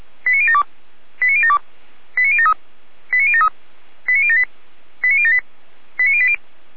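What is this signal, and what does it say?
Five-tone selective call (ZVEI), heard as quick runs of five tones, tones of about 70 ms. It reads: 89072 89072 89073 89073 89078 89078 89080
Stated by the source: ZVEI five-tone call 89050 received by SDR.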